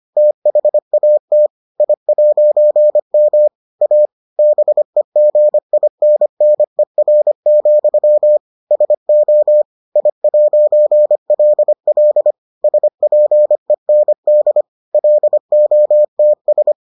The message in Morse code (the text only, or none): THAT I'M A BEGINNER, SO I'LL SPEND LOTS